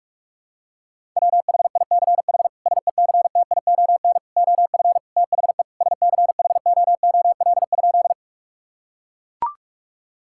Morse code, heard as a keyed tone, about 42 wpm, 700 Hz. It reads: WHICH SECTION OF THE SCHOOL?
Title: Sentences from Top 1000 Words - Review Part 5 - 45wpm